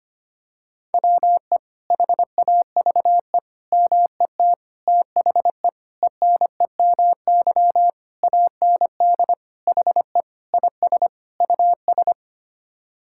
Morse code, heard as a keyed tone, about 25 wpm, 700 Hz. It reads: WE HAVE MET THE ENEMY AND HE IS US
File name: Sets of 9 Words - 25wpm